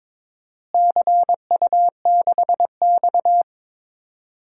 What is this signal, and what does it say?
Morse code, heard as a keyed tone, about 22 wpm, 700 Hz. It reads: CU6X
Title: DXCC-100 Call Signs - 4 Characters - Repeated 3x - 22wpm